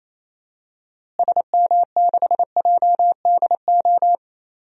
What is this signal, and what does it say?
Morse code, heard as a keyed tone, about 28 wpm, 700 Hz. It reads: SM6JDO
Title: DXCC-100 Call Signs - 6 Characters - Repeated 3x - 28wpm